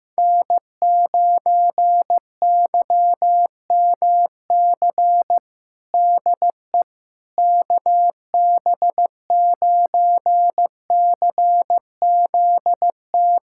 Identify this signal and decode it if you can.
Morse code, heard as a keyed tone, about 15 wpm, 700 Hz. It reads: N9YMC DE KB9CZT
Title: USA Callsign DE Callsign 15wpm